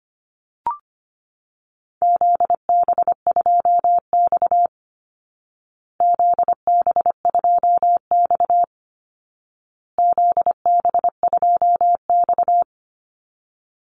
Morse code, heard as a keyed tone, about 25 wpm, 700 Hz. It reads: ZB2X ZB2X ZB2X
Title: Call Signs Repeated 2x - 25wpm